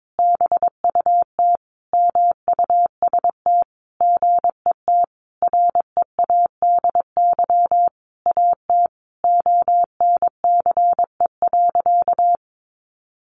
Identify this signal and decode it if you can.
Morse code, heard as a keyed tone, about 22 wpm, 700 Hz. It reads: BUT MUST GET READY AT ONCE.
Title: Sentences from Top 1100 Words 22wpm